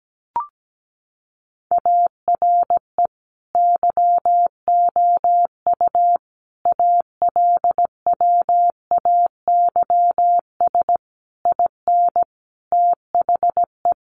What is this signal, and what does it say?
Morse code, heard as a keyed tone, about 17 wpm, 700 Hz. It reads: ARE YOU ALWAYS IN THE
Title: Sentences from Top 1400 Words 17wpm